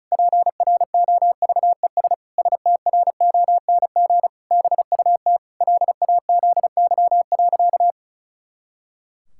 Morse code, heard as a keyed tone, about 36 wpm, 700 Hz. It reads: PROVES STRONG BUT LAZY.
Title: Pangram Challenge v2 - 35wpm